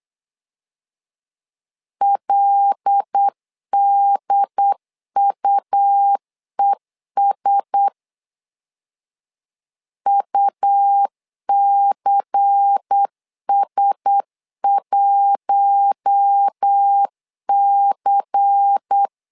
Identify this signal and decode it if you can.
Morse code, heard as a keyed tone, about 8 wpm, 800 Hz. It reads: LDUES UCS1C
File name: LEZIONI / 5 LBVC / 5 LBVC 8wpm 2935